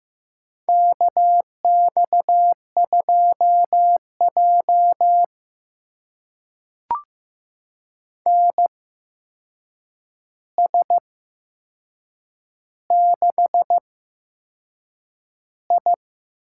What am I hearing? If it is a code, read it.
Morse code, heard as a keyed tone, about 15 wpm, spaced as for 6 wpm, 700 Hz. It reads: KX2J NS6I
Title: USA Call Signs - 4 Characters - 8x Character Spacing - 15wpm